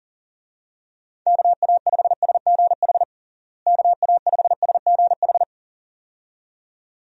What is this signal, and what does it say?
Morse code, heard as a keyed tone, about 40 wpm, 700 Hz. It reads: KA5SGH KA5SGH
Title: USA Call Signs - 6 Characters - Repeated 2x - 40wpm